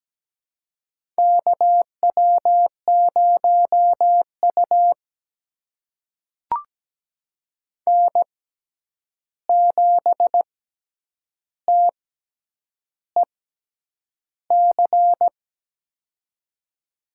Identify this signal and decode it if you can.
Morse code, heard as a keyed tone, about 17 wpm, spaced as for 8 wpm, 700 Hz. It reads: KW0U N7TEC
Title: USA Call Signs - 6x Character Spacing - 17wpm